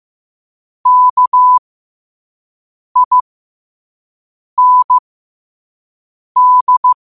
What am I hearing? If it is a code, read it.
Morse code, heard as a keyed tone, about 15 wpm, spaced as for 5 wpm, 1000 Hz. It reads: KIND
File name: qotdfiles / 5wpm